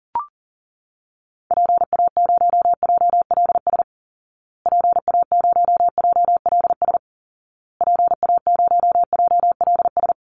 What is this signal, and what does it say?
Morse code, heard as a keyed tone, about 40 wpm, 700 Hz. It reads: PA0JLS PA0JLS PA0JLS
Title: Call Signs Repeated 2x - 40wpm